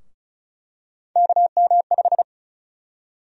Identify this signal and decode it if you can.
Morse code, heard as a keyed tone, about 36 wpm, 700 Hz. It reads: KM5
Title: USA Call Sign Prefixes - 35wpm